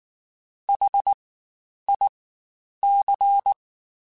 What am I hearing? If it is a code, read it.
Morse code, heard as a keyed tone, about 19 wpm, spaced as for 9 wpm, 800 Hz. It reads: HIC
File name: Spaced Alphas / Spaced Alphas mp3s / Hs sp @18